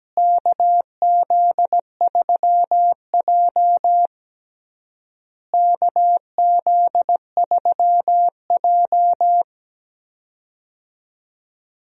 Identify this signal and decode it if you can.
Morse code, heard as a keyed tone, about 17 wpm, 700 Hz. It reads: KZ3J KZ3J
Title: USA Call Signs - 4 Characters - Repeated 3x - 17wpm